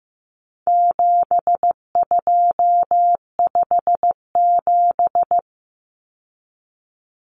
Morse code, heard as a keyed tone, about 15 wpm, 700 Hz. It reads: 7257
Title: Numbers - 4-Digit 15wpm